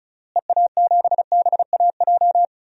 Morse code, heard as a keyed tone, about 36 wpm, 700 Hz. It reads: EA7BAJ
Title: DXCC-100 Call Signs - 6 Characters - Repeated 3x - 35wpm